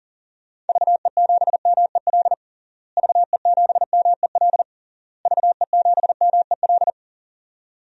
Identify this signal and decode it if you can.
Morse code, heard as a keyed tone, about 40 wpm, 700 Hz. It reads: VE7MEL VE7MEL VE7MEL
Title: DXCC-100 Call Signs - 6 Characters - Repeated 3x - 40wpm